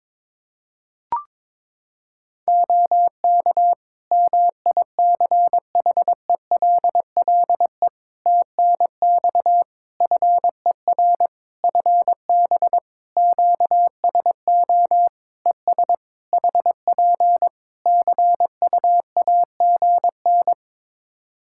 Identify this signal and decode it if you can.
Morse code, heard as a keyed tone, about 22 wpm, 700 Hz. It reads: OK MICHELLE TNX FER FB QSO ES HP CUAGN
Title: Practice QSO - Exchange 4 - Ending - 22wpm